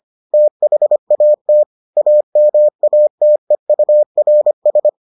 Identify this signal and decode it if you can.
Morse code, heard as a keyed tone, about 25 wpm, 600 Hz. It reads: THAT AMATEURS